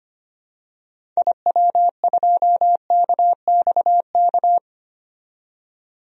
Morse code, heard as a keyed tone, about 25 wpm, 700 Hz. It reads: IW2KXK
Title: DXCC-100 Call Signs - 6 Characters - 25wpm